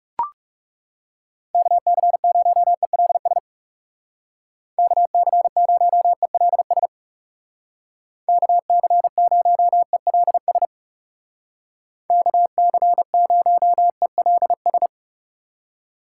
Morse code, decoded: KC0ELS KC0ELS KC0ELS KC0ELS

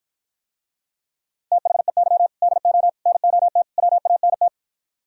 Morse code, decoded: THEY DO NOT WANT

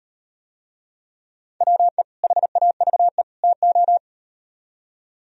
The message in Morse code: WE HAVE TO

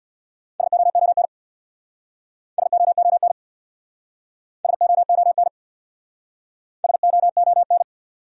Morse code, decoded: SOON SOON SOON SOON

